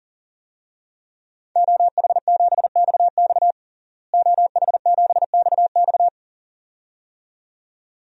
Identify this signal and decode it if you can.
Morse code, heard as a keyed tone, about 37 wpm, 700 Hz. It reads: OH7XX OH7XX
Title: DXCC-100 Call Signs - 5 Characters - Repeated 2x - 40wpm